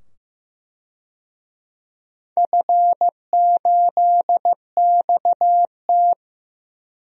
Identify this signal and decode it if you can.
Morse code, heard as a keyed tone, about 15 wpm, 700 Hz. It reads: F8XT
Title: DXCC-100 Call Signs - 4 Characters - 15wpm